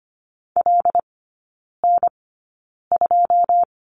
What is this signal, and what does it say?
Morse code, heard as a keyed tone, about 25 wpm, spaced as for 9 wpm, 700 Hz. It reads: LN2
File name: course / 25:10wpm / 093 Lesson 16e - Letter c - Three Characters 25wpm